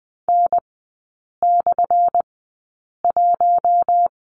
Morse code, heard as a keyed tone, about 20 wpm, spaced as for 8 wpm, 700 Hz. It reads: N/1
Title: 178 Lesson 29e - Slash - Three Characters 20wpm